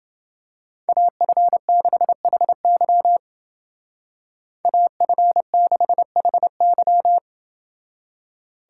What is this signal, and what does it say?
Morse code, heard as a keyed tone, about 29 wpm, 700 Hz. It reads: AF6HY AF6HY